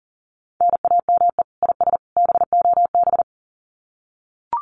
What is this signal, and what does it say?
Morse code, heard as a keyed tone, about 40 wpm, 700 Hz. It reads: NAME IS BOB
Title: Common QSO Phrases 40wpm